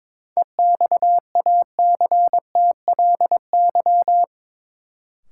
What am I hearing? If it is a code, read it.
Morse code, heard as a keyed tone, about 22 wpm, 700 Hz. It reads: EXACTLY
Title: Sentences from Top 900 Words - New Word Drill - 22wpm